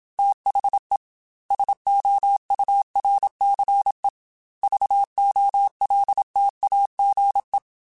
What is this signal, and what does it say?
Morse code, heard as a keyed tone, about 27 wpm, 800 Hz. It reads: THE SOURCE VOLTAGE
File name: Technical Subjects / Technical Subjects-mp3s / XC @25